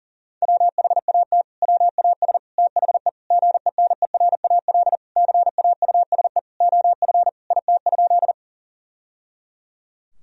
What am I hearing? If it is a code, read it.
Morse code, heard as a keyed tone, about 40 wpm, 700 Hz. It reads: WHAT WAS THE GENERAL CAUSE OF IT?
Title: Sentences from Top 1000 Words - Review Part 3 - 40wpm